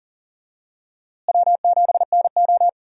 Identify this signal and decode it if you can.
Morse code, heard as a keyed tone, about 40 wpm, 700 Hz. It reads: W7NO
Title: USA Call Signs - 4 Characters - Repeated 2x - 40wpm